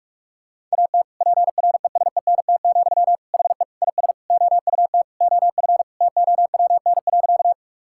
Morse code, heard as a keyed tone, about 42 wpm, 700 Hz. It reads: AT PRESENT, HE IS OUT OF TOWN.
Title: Sentences from Top 1000 Words - Review Part 5 - 45wpm